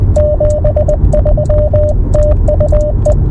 Morse code, heard as a keyed tone, about 20 wpm, 600 Hz, with radio noise. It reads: 73 TU E